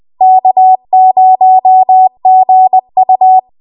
Morse code, heard as a keyed tone, about 20 wpm, 750 Hz. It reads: K0GU
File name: Archive / 20 WPM / 20WPM